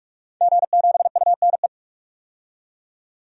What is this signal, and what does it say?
Morse code, heard as a keyed tone, about 46 wpm, 700 Hz. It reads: G7UNE
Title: DXCC-100 Call Signs - 5 Characters - 45wpm